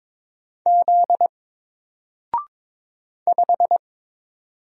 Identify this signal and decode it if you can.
Morse code, heard as a keyed tone, about 22 wpm, 700 Hz. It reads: Z 5
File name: Single-Letter-Number-22wpm